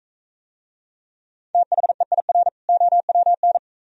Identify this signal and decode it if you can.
Morse code, heard as a keyed tone, about 44 wpm, 690 Hz. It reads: THEIR OWN